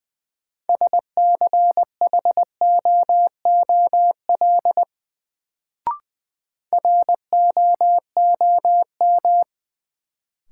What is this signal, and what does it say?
Morse code, heard as a keyed tone, about 21 wpm, 700 Hz. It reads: SCHOOL ROOM
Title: Sentences from Top 300 Words - New Word Drill - 20wpm